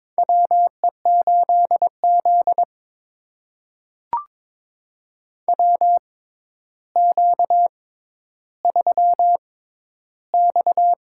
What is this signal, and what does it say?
Morse code, heard as a keyed tone, about 21 wpm, spaced as for 11 wpm, 700 Hz. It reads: WE8Z WQ3X
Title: USA Call Signs - 4 Characters - 6x Character Spacing - 22wpm